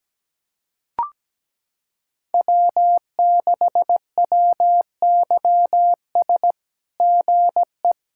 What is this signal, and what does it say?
Morse code, heard as a keyed tone, about 17 wpm, 700 Hz. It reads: W6WYS GE
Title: SOTA QSO Practice 17wpm